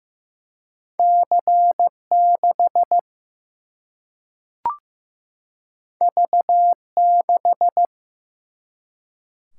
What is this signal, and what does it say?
Morse code, heard as a keyed tone, about 15 wpm, 700 Hz. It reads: C 6 V 6
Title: Single-Letter Single-Number 15wpm